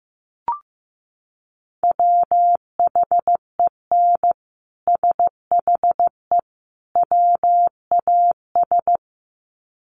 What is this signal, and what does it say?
Morse code, heard as a keyed tone, about 15 wpm, 700 Hz. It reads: WHEN SHE WAS
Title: Sets of 3 Words 15wpm